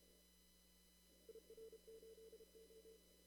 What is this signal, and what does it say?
Morse code, heard as a keyed tone, about 30 wpm, 450 Hz. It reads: IR8O
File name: CW / kt5j wpx cw-op-k5pi